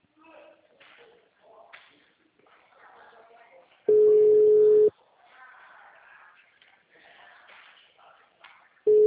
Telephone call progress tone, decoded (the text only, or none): ringback tone